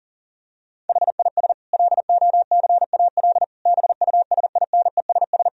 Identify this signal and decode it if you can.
Morse code, heard as a keyed tone, about 37 wpm, 700 Hz. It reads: HIS LOCAL BUSINESS
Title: Sentences from Top 1000 Words - Review Part 3 - 40wpm